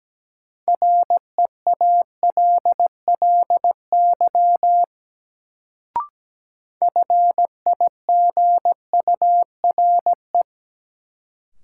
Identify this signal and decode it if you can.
Morse code, heard as a keyed tone, about 17 wpm, 700 Hz. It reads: REALLY FIGURE
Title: Sets of 4 Words - New Word Drill - 17wpm